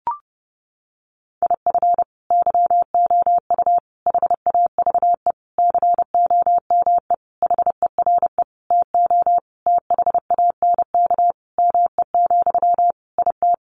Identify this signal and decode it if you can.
Morse code, heard as a keyed tone, about 28 wpm, 700 Hz. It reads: IF YOU HAVE COME HERE TO THANK ME, IT
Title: Sentences from Top 800 Words 30wpm